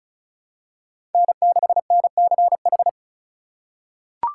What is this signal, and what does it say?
Morse code, heard as a keyed tone, about 34 wpm, 700 Hz. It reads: N6NCH
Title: USA Call Signs - 5 Characters - 3x Character Spacing - 35wpm